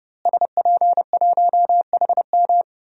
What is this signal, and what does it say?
Morse code, heard as a keyed tone, about 30 wpm, 700 Hz. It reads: SP1HM
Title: DXCC-100 Call Signs - 5 Characters - Repeated 2x - 30wpm